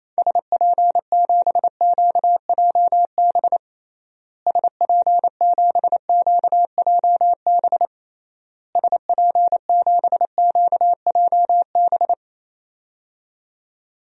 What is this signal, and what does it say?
Morse code, heard as a keyed tone, about 28 wpm, 700 Hz. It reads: SP7QJB SP7QJB SP7QJB
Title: DXCC-100 Call Signs - 6 Characters - Repeated 3x - 28wpm